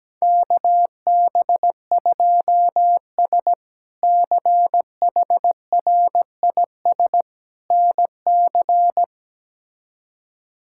Morse code, heard as a keyed tone, about 17 wpm, 700 Hz. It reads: KB2S CHRIS NC
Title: K1USN SST QSO Practice - 17wpm